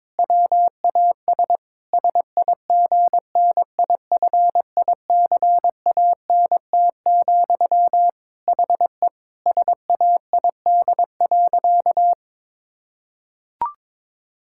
Morse code, heard as a keyed tone, about 22 wpm, 700 Hz. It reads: WAS SIGNIFICANT, HE SAID.
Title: Sentences from Top 1000 Words - Review Part 6 - 22wpm